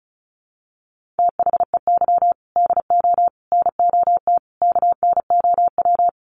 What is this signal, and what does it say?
Morse code, heard as a keyed tone, about 34 wpm, 700 Hz. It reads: THEY DO NOT KNOW
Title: Sets of 4 Words 35wpm